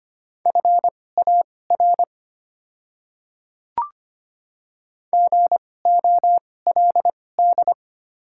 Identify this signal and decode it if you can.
Morse code, heard as a keyed tone, about 25 wpm, spaced as for 18 wpm, 700 Hz. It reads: FAR GOLD
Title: 2x Character Spacing - Top 500 Words - 25wpm